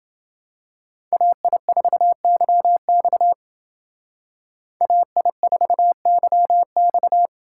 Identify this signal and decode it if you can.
Morse code, heard as a keyed tone, about 28 wpm, 700 Hz. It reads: AI4YX AI4YX